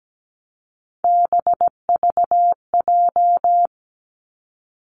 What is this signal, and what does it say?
Morse code, heard as a keyed tone, about 17 wpm, 700 Hz. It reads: BVJ
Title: Three Challenging Letter-Number Combinations 17wpm